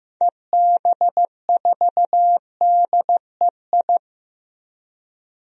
Morse code, heard as a keyed tone, about 15 wpm, 700 Hz. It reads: EB4DEI
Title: DXCC-100 Call Signs - 6 Characters - Repeated 3x - 15wpm